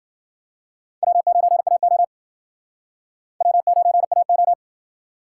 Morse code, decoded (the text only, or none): W9AO W9AO